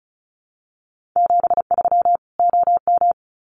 Morse code, heard as a keyed tone, about 34 wpm, 700 Hz. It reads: 73 OM
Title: Common QSO Phrases 35wpm